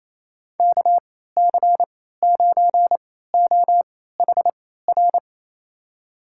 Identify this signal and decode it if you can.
Morse code, heard as a keyed tone, about 28 wpm, spaced as for 16 wpm, 700 Hz. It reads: KC9OHR
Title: USA Call Signs - 6 Characters - 3x Character Spacing - 28wpm